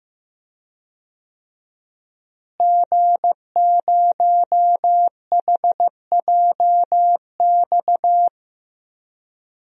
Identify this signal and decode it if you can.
Morse code, heard as a keyed tone, about 15 wpm, 700 Hz. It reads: G0HJX
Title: DXCC-100 Call Signs - 5 Characters - Repeated 2x - 15wpm